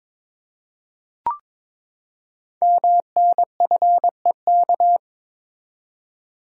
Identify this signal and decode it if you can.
Morse code, heard as a keyed tone, about 22 wpm, 700 Hz. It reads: MNFEK
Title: Code Groups - 5 Letters - 22wpm